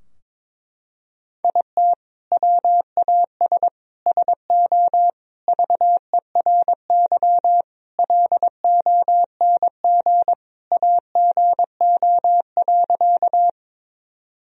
Morse code, heard as a keyed tone, about 22 wpm, 700 Hz. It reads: IT WAS SO VERY LONG AGO.
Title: Sentences from Top 1000 Words - Review Part 3 - 22wpm